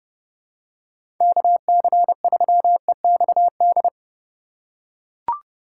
Morse code, heard as a keyed tone, about 30 wpm, 700 Hz. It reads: KC3EXD
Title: USA Call Signs - 6 Characters - 4x Character Spacing - 30wpm